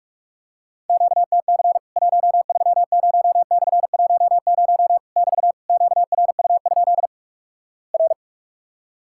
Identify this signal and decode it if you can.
Morse code, heard as a keyed tone, about 46 wpm, 700 Hz. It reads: QTC 130/10 <BT> QRU? R
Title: Worked All Europe DX Contest - QTC Exchange - 45wpm